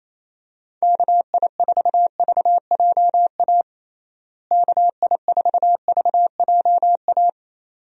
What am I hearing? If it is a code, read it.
Morse code, heard as a keyed tone, about 28 wpm, 700 Hz. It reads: KI4VJA KI4VJA
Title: USA Call Signs - 6 Characters - Repeated 2x - 28wpm